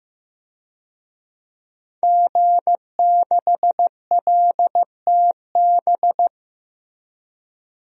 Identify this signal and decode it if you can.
Morse code, heard as a keyed tone, about 15 wpm, 700 Hz. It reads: G6LTB
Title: DXCC-100 Call Signs - 5 Characters - Repeated 2x - 15wpm